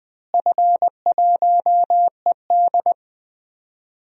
Morse code, heard as a keyed tone, about 21 wpm, 700 Hz. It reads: F1ED
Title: DXCC-100 Call Signs - 4 Characters - Repeated 3x - 20wpm